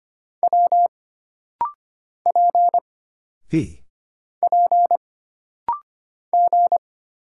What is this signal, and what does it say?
Morse code, heard as a keyed tone, about 25 wpm, 700 Hz. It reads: W P P G
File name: Single Character (Letter:Number:Punctuation:Prosign) - Rapid-Fire - 25wpm